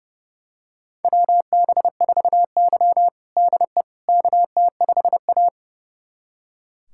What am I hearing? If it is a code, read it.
Morse code, heard as a keyed tone, about 30 wpm, 700 Hz. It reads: WB4Y DE KT5A
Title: USA Callsign DE Callsign - 4 Characters - 30wpm